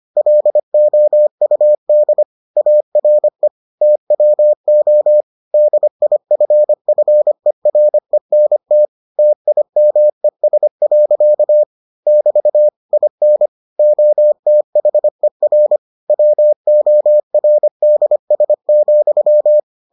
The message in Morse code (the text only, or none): LOUD ARE TWO DIFFERENT TIMES. <BT> IN OTHER WORDS,